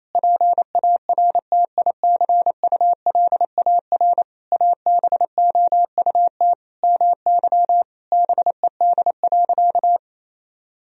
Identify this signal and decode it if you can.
Morse code, heard as a keyed tone, about 28 wpm, 700 Hz. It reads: PARTICULAR ABOUT MY BED.
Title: Sentences from Top 1000 Words - Review Part 4 - 28wpm